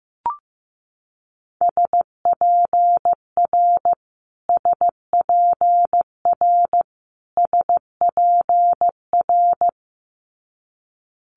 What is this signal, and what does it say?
Morse code, heard as a keyed tone, about 15 wpm, 700 Hz. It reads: SPR SPR SPR
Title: Multi-letter Phonemes Repeated 3x 15wpm